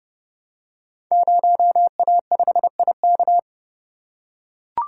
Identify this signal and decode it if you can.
Morse code, heard as a keyed tone, about 30 wpm, 700 Hz. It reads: 0A5IK
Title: Code Groups - 5 Letter:Numbers - 30wpm